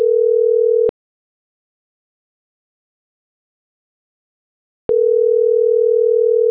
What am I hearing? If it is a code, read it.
Ringback tone (North America), 440 + 480 Hz.